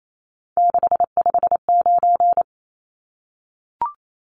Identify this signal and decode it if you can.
Morse code, heard as a keyed tone, about 28 wpm, 700 Hz. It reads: 659
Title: Numbers - 3-Digit 28wpm